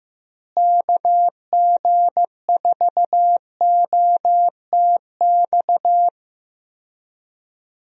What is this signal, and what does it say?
Morse code, heard as a keyed tone, about 15 wpm, 700 Hz. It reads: KG4OTX
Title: USA Call Signs - 6 Characters - Repeated 3x - 15wpm